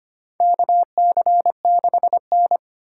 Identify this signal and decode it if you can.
Morse code, heard as a keyed tone, about 25 wpm, 700 Hz. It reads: KC6N
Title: USA Call Signs - 2x Character Spacing - 25wpm